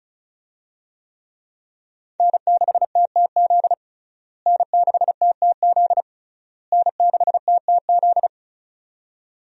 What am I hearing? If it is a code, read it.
Morse code, heard as a keyed tone, about 34 wpm, 700 Hz. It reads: N6TTZ N6TTZ N6TTZ
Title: USA Call Signs - 5 Characters - Repeated 3x - 35wpm